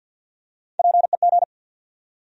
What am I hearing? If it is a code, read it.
Morse code, heard as a keyed tone, about 53 wpm, 700 Hz. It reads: PEG
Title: USA Common Names - Max 3 Letters - 50wpm